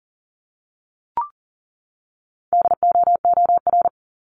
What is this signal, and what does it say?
Morse code, heard as a keyed tone, about 40 wpm, 700 Hz. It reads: DOOR
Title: Top 300 Words 40wpm